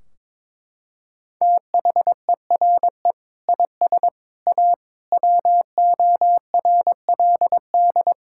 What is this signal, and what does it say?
Morse code, heard as a keyed tone, about 22 wpm, 700 Hz. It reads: THERE IS A WORLD
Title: Sentences from Top 1600 Words 22wpm